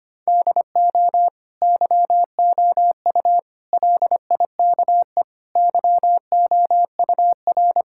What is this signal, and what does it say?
Morse code, heard as a keyed tone, about 25 wpm, 700 Hz. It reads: DO YOU LIKE YOUR
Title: Sentences from Top 1000 Words - Review Part 6 - 25wpm